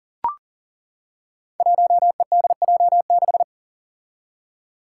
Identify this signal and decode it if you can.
Morse code, heard as a keyed tone, about 40 wpm, 700 Hz. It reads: 1EDJ6
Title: Five Challenging Letter-Number Combinations 40wpm